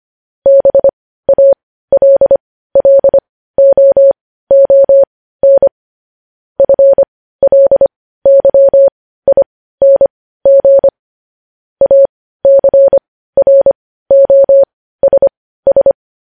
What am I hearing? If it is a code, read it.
Morse code, heard as a keyed tone, about 25 wpm, spaced as for 15 wpm, 550 Hz. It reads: BALLOON FLYING ACROSS
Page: news headlines in morse code at 15 wpm